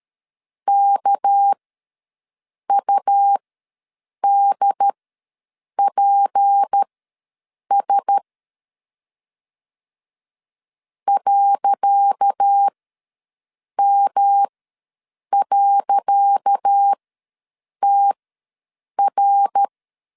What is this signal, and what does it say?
Morse code, heard as a keyed tone, about 13 wpm, spaced as for 7 wpm, 800 Hz. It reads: KUDPS .M.TR
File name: LEZIONI / 10 SIGNS / 10C DOT ws 12wpm 3548